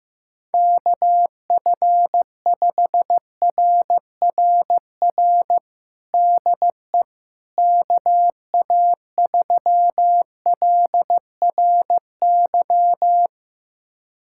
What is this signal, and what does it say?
Morse code, heard as a keyed tone, about 15 wpm, 700 Hz. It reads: KF5RRR DE KA3LRY
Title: USA Callsign DE Callsign - 6 Characters - 15wpm